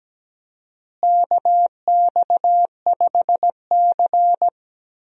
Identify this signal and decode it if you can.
Morse code, heard as a keyed tone, about 17 wpm, 700 Hz. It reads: KX5C
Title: USA Call Signs - 4 Characters - Repeated 2x - 17wpm